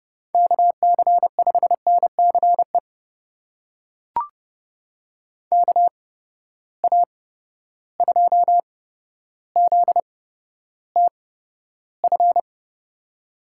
Morse code, heard as a keyed tone, about 31 wpm, spaced as for 13 wpm, 700 Hz. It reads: KC5NCE KA2ZTF
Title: USA Call Signs - 6 Characters - 8x Character Spacing - 30wpm